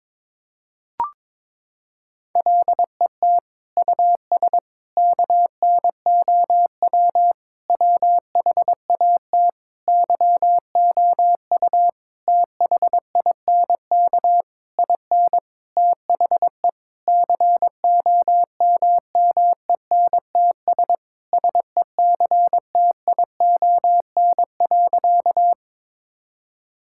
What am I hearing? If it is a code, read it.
Morse code, heard as a keyed tone, about 22 wpm, 700 Hz. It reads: LET US KNOW WHAT YOU THINK IN THE COMMENTS SECTION.
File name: Sentences with 9-Word Phrases - 22wpm